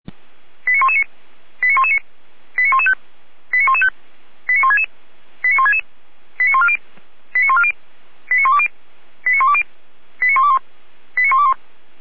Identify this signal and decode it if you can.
Five-tone selective call (ZVEI), heard as quick runs of five tones, tones of about 80 ms. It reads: 89119 89119 89115 89115 89166 89166 89140 89140 89120 89120 89121 89121